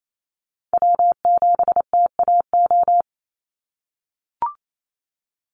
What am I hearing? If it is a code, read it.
Morse code, heard as a keyed tone, about 28 wpm, 700 Hz. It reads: W7TAO
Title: NA SOTA Chaser Callsigns 28wpm